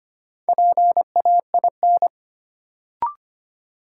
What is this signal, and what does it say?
Morse code, heard as a keyed tone, about 25 wpm, 700 Hz. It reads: PAIN
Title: Sentences from Top 1600 Words - New Word Drill Alt - 25wpm